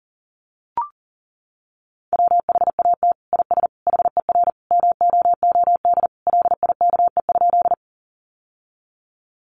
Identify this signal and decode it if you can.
Morse code, heard as a keyed tone, about 40 wpm, 700 Hz. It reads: WHAT IS HER MOOD LIKE?
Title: Sentences from Top 1100 Words 40wpm